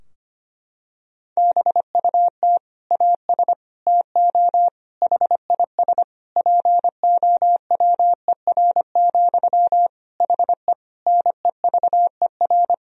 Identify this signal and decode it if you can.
Morse code, heard as a keyed tone, about 25 wpm, 700 Hz. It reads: BUT AS TO HIS POWER, HE NEVER
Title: Sentences from Top 1000 Words - Review Part 2 - 25wpm